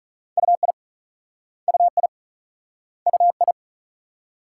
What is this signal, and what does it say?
Morse code, heard as a keyed tone, about 44 wpm, 700 Hz. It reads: UI UI UI
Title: Speed-Racing Multi-letter Phonemes - 35wpm